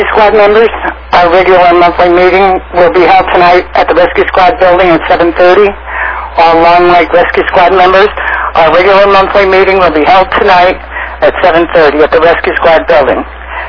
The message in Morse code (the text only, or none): none